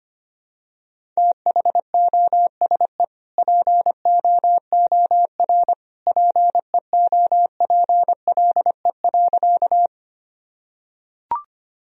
Morse code, heard as a keyed tone, about 25 wpm, 700 Hz. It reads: THOSE POOR PEOPLE.
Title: Sentences from Top 1000 Words - Review Part 1 - 25wpm